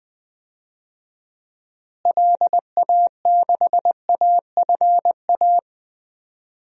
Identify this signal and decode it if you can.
Morse code, heard as a keyed tone, about 20 wpm, 700 Hz. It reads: LA6AFA
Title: DXCC-100 Call Signs - 6 Characters - Repeated 2x - 20wpm